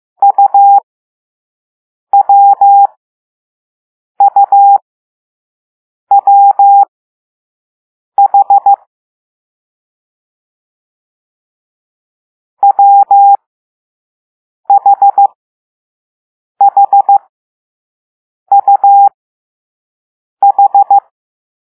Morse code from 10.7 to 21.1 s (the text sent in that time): WHHUH